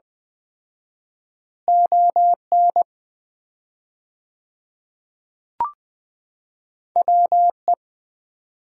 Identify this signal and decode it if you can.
Morse code, heard as a keyed tone, about 21 wpm, 700 Hz. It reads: ON WE
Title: 4x Word-Spacing - Sets of 2 Words - 20wpm